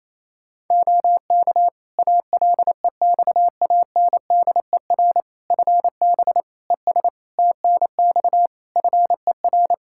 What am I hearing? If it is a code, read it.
Morse code, heard as a keyed tone, about 28 wpm, 700 Hz. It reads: OK ALEXANDER FB ES TNX FER
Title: Practice QSO - Exchange 2 - Info - 28wpm